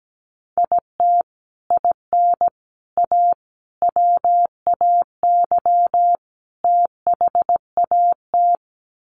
Morse code, heard as a keyed tone, about 17 wpm, 700 Hz. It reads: IT IN A WAY THAT
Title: Sets of 5 Words Encore2 17wpm